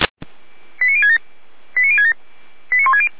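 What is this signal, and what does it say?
Five-tone selective call (ZVEI), heard as quick runs of five tones, tones of about 70 ms. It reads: 89067 89067 89160
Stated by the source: ZVEI five-tone selective call 89560 received by SDR.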